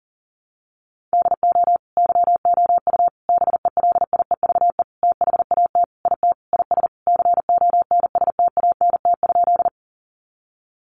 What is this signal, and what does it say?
Morse code, heard as a keyed tone, about 40 wpm, 700 Hz. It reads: DO YOU BELIEVE THAT IT IS CONSTANT?
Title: Sentences from Top 1500 Words 40wpm